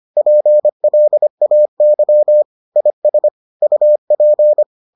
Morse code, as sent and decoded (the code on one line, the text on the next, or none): .--. .-.. .- -.-- / .. ... / ..- .--.
PLAY IS UP